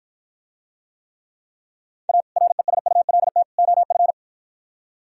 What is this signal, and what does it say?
Morse code, heard as a keyed tone, about 52 wpm, 700 Hz. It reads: A RESULT OF